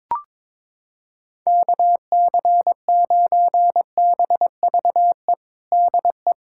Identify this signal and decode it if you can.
Morse code, heard as a keyed tone, about 22 wpm, 700 Hz. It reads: KC9BVE DE
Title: USA Callsign DE Callsign - 6 Characters - 22wpm